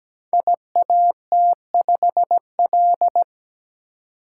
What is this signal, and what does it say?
Morse code, heard as a keyed tone, about 17 wpm, 700 Hz. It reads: IAT5L
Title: Code Groups - 5 Letter:Numbers - 17wpm